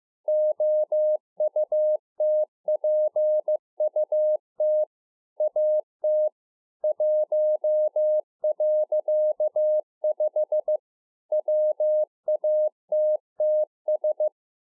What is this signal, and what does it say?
Morse code, heard as a keyed tone, about 15 wpm, 610 Hz. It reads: OUTPUT AT 1.5 WATTS
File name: 2011-04-mp3 / A SMALL TRANCEIVER 2 15WPM